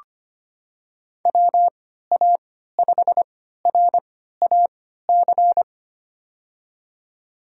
Morse code, heard as a keyed tone, about 25 wpm, spaced as for 14 wpm, 700 Hz. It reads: WA5RAC
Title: USA Call Signs - 6 Characters - 3x Character Spacing - 25wpm